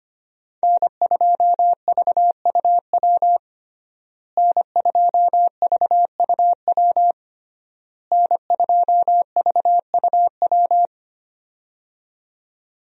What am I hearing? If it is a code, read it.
Morse code, heard as a keyed tone, about 25 wpm, 700 Hz. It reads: N2VUW N2VUW N2VUW
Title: USA Call Signs - 5 Characters - Repeated 3x - 25wpm